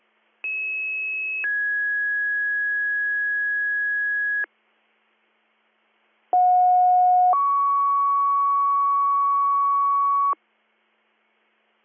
Two-tone page, heard as a one-second tone, then a three-second tone, about 2600 then 1650 Hz, about 730 then 1120 Hz. It is heard from 0.4 to 10.3 s.